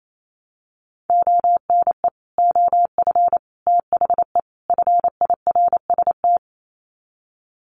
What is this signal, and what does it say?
Morse code, heard as a keyed tone, about 28 wpm, 700 Hz. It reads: ONE OF THE FIRST
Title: Sets of 4 Words 28wpm